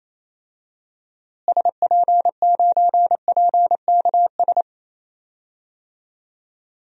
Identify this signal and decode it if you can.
Morse code, heard as a keyed tone, about 28 wpm, 700 Hz. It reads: SP9PKS
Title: DXCC-100 Call Signs - 6 Characters - Repeated 2x - 28wpm